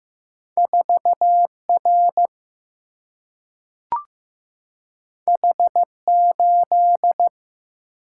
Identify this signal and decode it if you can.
Morse code, heard as a keyed tone, about 15 wpm, 700 Hz. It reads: 4R H8
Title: Single-Letter Single-Number 15wpm